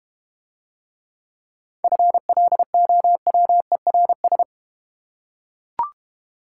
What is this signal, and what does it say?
Morse code, heard as a keyed tone, about 31 wpm, 700 Hz. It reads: FLOWERS
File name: Easy Words - Speed-Racing with Hints - 25wpm